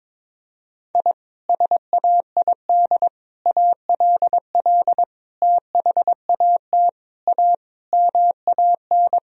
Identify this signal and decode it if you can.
Morse code, heard as a keyed tone, about 22 wpm, 700 Hz. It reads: I SAID ALL THAT A MAN
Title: Sentences from Top 1000 Words - Review Part 2 - 22wpm